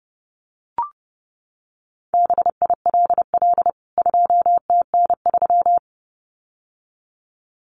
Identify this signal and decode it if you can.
Morse code, heard as a keyed tone, about 30 wpm, 700 Hz. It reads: BILL 2TN3
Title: CWT Exchange 30wpm